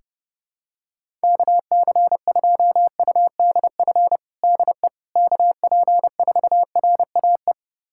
Morse code, heard as a keyed tone, about 32 wpm, 700 Hz. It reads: KC2UDF DE KP4RAE
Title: USA Callsign DE Callsign - 6 Characters - 30wpm